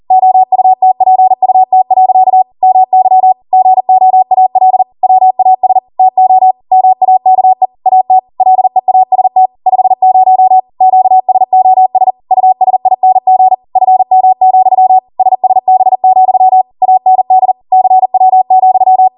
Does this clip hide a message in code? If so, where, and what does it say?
Yes: Morse code from 0.1 to 19.1 s, heard as a keyed tone, about 40 wpm, 750 Hz. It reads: OUTPUT. MY GOAL WAS TO MAKE AT LEAST 50 QSOS USING FM, SSB, AND CW,